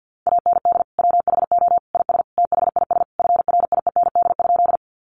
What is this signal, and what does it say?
Morse code, heard as a keyed tone, about 50 wpm, 700 Hz. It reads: AND WHO IS THIS FRIEND?
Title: Sentences from Top 300 Words 50wpm